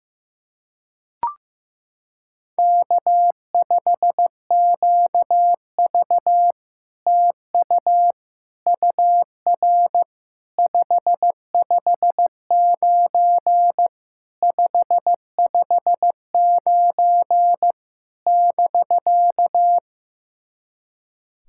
Morse code, heard as a keyed tone, about 16 wpm, 700 Hz. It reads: K5QV TU UR 559 559 <BK>
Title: SOTA QSO Practice 15wpm